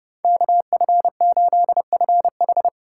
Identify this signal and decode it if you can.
Morse code, heard as a keyed tone, about 36 wpm, 700 Hz. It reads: KF8FH
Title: Speed Racing - USA 2x2 Call Signs 30wpm